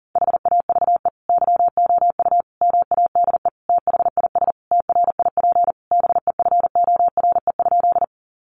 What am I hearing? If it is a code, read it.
Morse code, heard as a keyed tone, about 40 wpm, 700 Hz. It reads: HAVE YOU MADE THIS TRIP BEFORE?
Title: Sentences from Top 1300 Words 40wpm